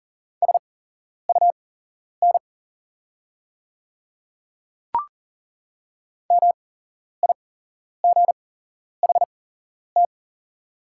Morse code, heard as a keyed tone, about 39 wpm, spaced as for 11 wpm, 690 Hz. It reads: SUN MIGHT
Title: 8x Character Spacing - Top 500 Words - 40wpm